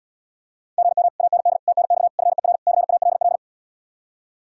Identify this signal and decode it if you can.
Morse code, heard as a keyed tone, about 80 wpm, 700 Hz. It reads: GM AND TNX FER CALL